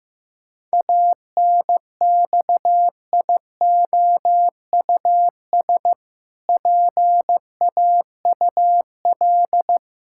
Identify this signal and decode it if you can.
Morse code, heard as a keyed tone, about 15 wpm, 700 Hz. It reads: ANXIOUS PAUL
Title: Pangram Challenge v2 - 15wpm